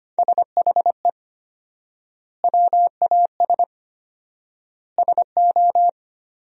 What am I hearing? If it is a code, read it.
Morse code, heard as a keyed tone, about 25 wpm, 700 Hz. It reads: SHE WAS SO
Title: 4x Word-Spacing - Sentences from Top 400 Words - 25wpm